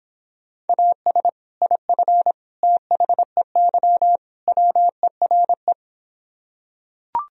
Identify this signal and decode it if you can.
Morse code, heard as a keyed tone, about 26 wpm, 700 Hz. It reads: AS IF THEY WERE